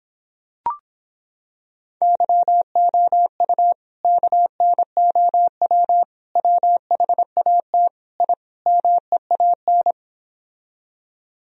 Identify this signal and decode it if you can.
Morse code, heard as a keyed tone, about 26 wpm, 700 Hz. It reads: YOU KNOW WHAT I MEAN